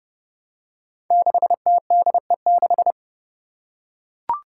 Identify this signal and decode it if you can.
Morse code, heard as a keyed tone, about 30 wpm, 700 Hz. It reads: 6TDE6
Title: Five Challenging Letter-Number Combinations 30wpm